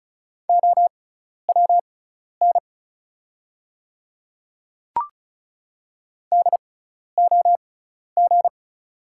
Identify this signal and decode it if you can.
Morse code, heard as a keyed tone, about 34 wpm, spaced as for 12 wpm, 690 Hz. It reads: OWN DOG
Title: 6x Character Spacing - Top 500 Words - 35wpm